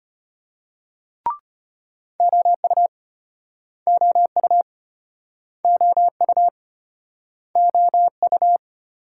Morse code, heard as a keyed tone, about 34 wpm, 700 Hz. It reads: OU OU OU OU